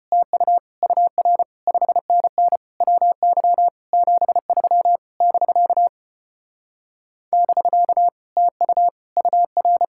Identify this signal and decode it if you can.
Morse code, heard as a keyed tone, about 33 wpm, 700 Hz. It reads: TU UR 5NN WY 73 <BK> <BK> TU UR